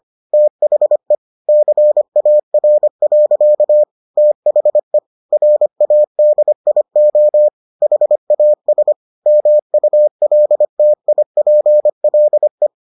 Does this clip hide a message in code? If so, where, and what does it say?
Yes: Morse code from 0.3 to 12.7 s, heard as a keyed tone, about 25 wpm, 600 Hz. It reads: THE CAR. THE RADIO HAS MULTIPLE